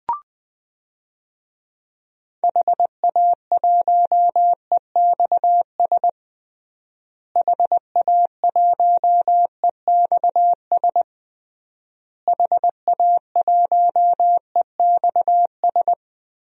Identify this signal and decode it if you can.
Morse code, heard as a keyed tone, about 20 wpm, 700 Hz. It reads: HA1EXS HA1EXS HA1EXS
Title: DXCC-100 Call Signs - 6 Characters - Repeated 3x - 20wpm